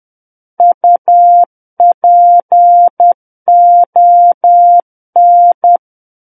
Morse code, heard as a keyed tone, about 10 wpm, 700 Hz. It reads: UPON
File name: Book One / 01-Chapter 1 - 10 WPM